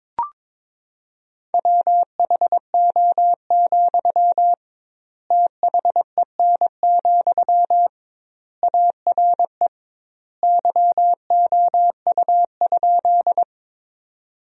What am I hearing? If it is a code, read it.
Morse code, heard as a keyed tone, about 21 wpm, 700 Hz. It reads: WHO, THEN, ARE YOU?
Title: 2x Word-Spacing - Sentences from Top 100 Words - 22wpm